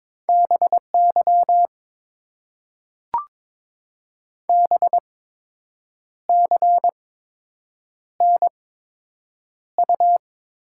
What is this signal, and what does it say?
Morse code, heard as a keyed tone, about 22 wpm, spaced as for 7 wpm, 700 Hz. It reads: BY BCNU
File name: Top 90 QSO Elements - 8x Character Spacing - 22wpm